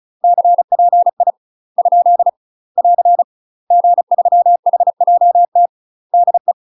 Morse code, heard as a keyed tone, about 35 wpm, spaced as for 30 wpm, 700 Hz. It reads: CPI? <AR> G3HJT DE
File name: UK MASTER QSOs TRAINING / 35WPM / UK MASTER QSOs 0014